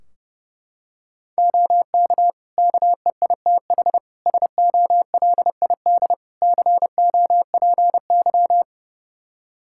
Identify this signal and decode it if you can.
Morse code, heard as a keyed tone, about 31 wpm, 700 Hz. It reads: OK KEITH SOLID COPY
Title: Practice QSO - Exchange 3 - More Info - 30wpm